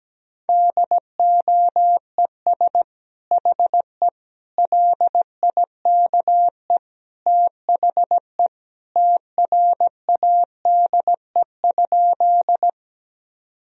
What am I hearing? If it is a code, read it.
Morse code, heard as a keyed tone, about 17 wpm, 700 Hz. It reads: DOES HE LIKE THE TRADE?
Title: Sentences from Top 1000 Words - Review Part 4 - 17wpm